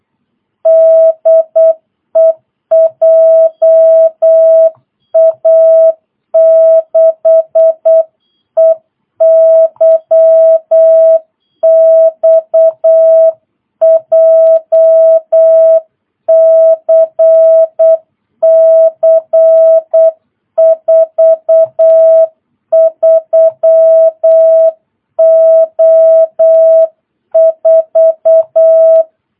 Morse code, heard as a keyed tone, about 8 wpm, 660 Hz. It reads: DE JA6EYX JCC 43O4
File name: jh8cbh / 2022cw